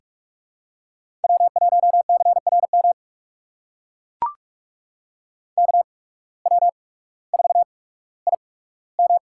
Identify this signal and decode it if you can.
Morse code, heard as a keyed tone, about 42 wpm, spaced as for 19 wpm, 700 Hz. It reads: W1CRM KW4IM
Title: USA Call Signs - 5 Characters - 8x Character Spacing - 45wpm